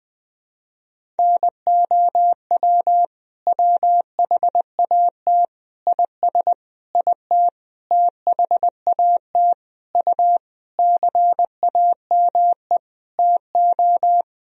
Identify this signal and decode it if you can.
Morse code, heard as a keyed tone, about 20 wpm, 700 Hz. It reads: NOW WHAT IS IT THAT U CAME TO
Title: Sentences from Top 200 Words and Abbreviations 20wpm